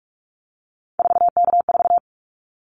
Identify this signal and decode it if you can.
Morse code, heard as a keyed tone, about 46 wpm, 700 Hz. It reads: 4K4
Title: Callsign Prefixes for Top 100 DXCC - 45wpm